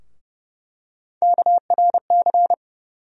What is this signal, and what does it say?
Morse code, heard as a keyed tone, about 29 wpm, 700 Hz. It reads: KRC
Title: USA Call Sign Postfixes - 3 Letters 30wpm